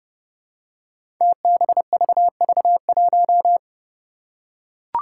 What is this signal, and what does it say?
Morse code, heard as a keyed tone, about 30 wpm, 700 Hz. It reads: TBVV1
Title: Five Challenging Letter-Number Combinations 30wpm